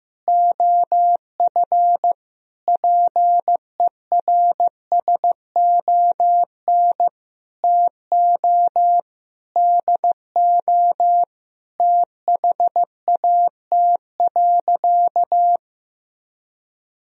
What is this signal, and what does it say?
Morse code, heard as a keyed tone, about 15 wpm, 700 Hz. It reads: OF PERSON TO DO THAT.
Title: Sentences from Top 1000 Words - Review Part 3 - 15wpm